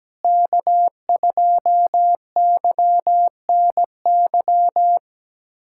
Morse code, heard as a keyed tone, about 17 wpm, 700 Hz. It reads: K2YNY
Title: USA Callsign DE Callsign - 5 Characters - 17wpm